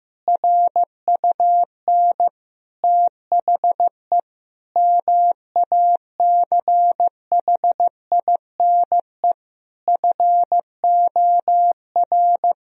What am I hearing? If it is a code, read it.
Morse code, heard as a keyed tone, about 16 wpm, 700 Hz. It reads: RUN THE MACHINE FOR
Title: Sentences from Top 1000 Words - Review Part 2 - 15wpm